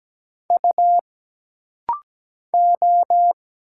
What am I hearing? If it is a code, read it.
Morse code, heard as a keyed tone, about 17 wpm, 700 Hz. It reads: U O
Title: Single Letters - Rapid-Fire 17wpm